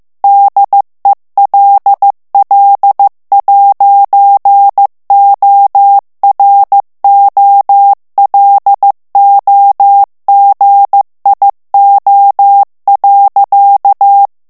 Morse code, heard as a keyed tone, about 15 wpm, 800 Hz. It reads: DELL'OROLOGIO.
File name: ITALIEN texte clair 1 vitesse 14 mots minute